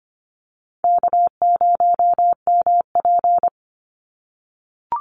Morse code, heard as a keyed tone, about 25 wpm, 700 Hz. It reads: K0MP
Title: NA SOTA Chaser Callsigns 25wpm